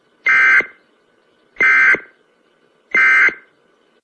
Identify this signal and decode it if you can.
Emergency Alert System (SAME) tones, heard as harsh buzzing bursts. It is heard from 0.2 to 3.4 s.
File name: emergency-broadcast